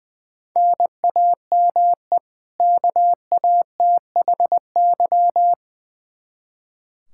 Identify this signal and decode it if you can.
Morse code, heard as a keyed tone, about 20 wpm, 700 Hz. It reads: NAME KATHY
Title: Practice QSO - Exchange 1 - Beginning - 20wpm